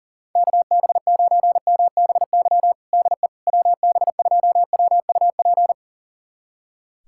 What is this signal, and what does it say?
Morse code, heard as a keyed tone, about 40 wpm, 700 Hz. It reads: KB9MBY DE WB2WUP
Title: USA Callsign DE Callsign - 6 Characters - 40wpm